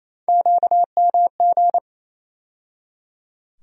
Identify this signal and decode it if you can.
Morse code, heard as a keyed tone, about 28 wpm, 700 Hz. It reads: QMG
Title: USA Call Sign Postfixes - 3 Letters 28wpm